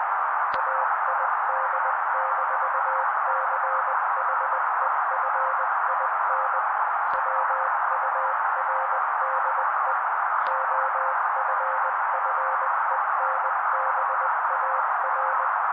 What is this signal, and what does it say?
Morse code, heard as a keyed tone, about 20 wpm, 530 Hz. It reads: AID-CHEFIN WURDE OFFENBAR
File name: daily / 18wpm noise